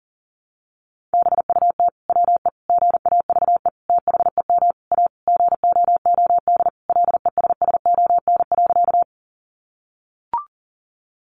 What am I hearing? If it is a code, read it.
Morse code, heard as a keyed tone, about 40 wpm, 700 Hz. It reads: BUT WE GAVE THEM A GOOD LESSON.
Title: Sentences from Top 1200 Words 40wpm